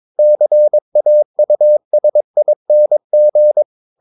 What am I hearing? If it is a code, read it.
Morse code, heard as a keyed tone, about 22 wpm, 600 Hz. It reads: CAUSING
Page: CW med Gnister